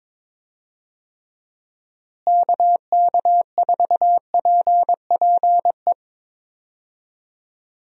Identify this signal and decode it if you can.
Morse code, heard as a keyed tone, about 22 wpm, 700 Hz. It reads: KK4PPE
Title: USA Call Signs - 6 Characters - Repeated 3x - 22wpm